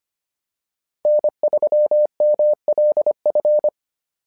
Morse code, heard as a keyed tone, about 25 wpm, 600 Hz. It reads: N3MLF
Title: SOTA QSO Practice 25wpm